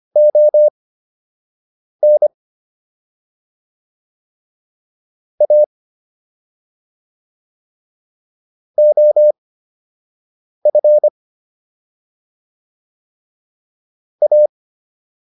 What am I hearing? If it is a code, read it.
Morse code, heard as a keyed tone, about 25 wpm, spaced as for 6 wpm, 600 Hz. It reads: ON A OF A